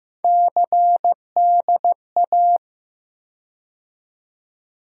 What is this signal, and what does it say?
Morse code, heard as a keyed tone, about 15 wpm, 700 Hz. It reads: CDA
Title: USA Call Sign Postfixes - 3 Letters 15wpm